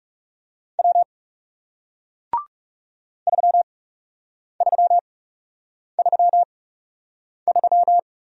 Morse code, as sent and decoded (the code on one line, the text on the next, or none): .-- / ...-- ...-- ...-- ...--
W 3333